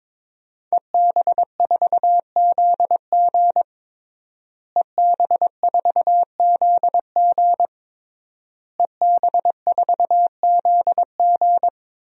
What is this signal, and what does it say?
Morse code, heard as a keyed tone, about 22 wpm, 700 Hz. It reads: EB4ZG EB4ZG EB4ZG
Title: DXCC-100 Call Signs - 5 Characters - Repeated 3x - 22wpm